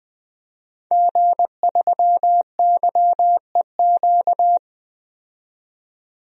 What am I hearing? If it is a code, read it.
Morse code, heard as a keyed tone, about 20 wpm, 700 Hz. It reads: G3YEQ
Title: DXCC-100 Call Signs - 5 Characters - 20wpm